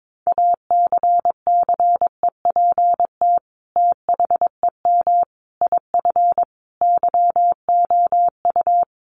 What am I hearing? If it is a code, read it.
Morse code, heard as a keyed tone, about 22 wpm, 700 Hz. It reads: ACCEPT THEM IF YOU
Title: Sentences from Top 1300 Words 22wpm